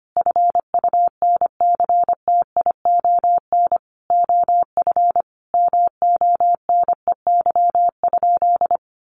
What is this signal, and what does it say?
Morse code, heard as a keyed tone, about 25 wpm, 690 Hz. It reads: FUNCTION OF MONEY?
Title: Sentences from Top 1100 Words 25wpm